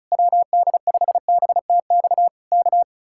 Morse code, heard as a keyed tone, about 33 wpm, 700 Hz. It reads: WD5BTX K
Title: Speed-Racing Common QSO Phrases Encore - 35wpm